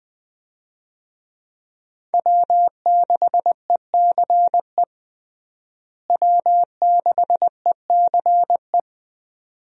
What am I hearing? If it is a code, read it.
Morse code, heard as a keyed tone, about 20 wpm, 700 Hz. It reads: W6ECE W6ECE
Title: USA Call Signs - 5 Characters - Repeated 2x - 20wpm